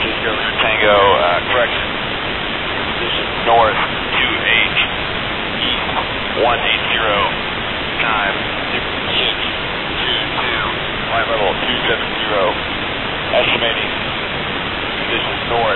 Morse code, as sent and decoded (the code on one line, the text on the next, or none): none